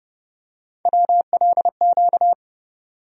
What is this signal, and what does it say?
Morse code, heard as a keyed tone, about 30 wpm, 700 Hz. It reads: WLQ
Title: USA Call Sign Postfixes - 3 Letters 30wpm